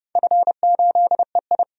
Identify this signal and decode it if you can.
Morse code, heard as a keyed tone, about 30 wpm, 700 Hz. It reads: F8EI
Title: DXCC-100 Call Signs - 4 Characters - 30wpm